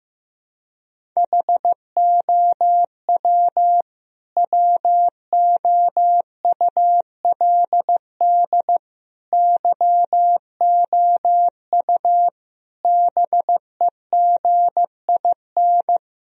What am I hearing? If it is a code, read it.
Morse code, heard as a keyed tone, about 15 wpm, 700 Hz. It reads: HOW WOULD YOU BEGIN
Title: Sentences from Top 1000 Words - Review Part 1 - 15wpm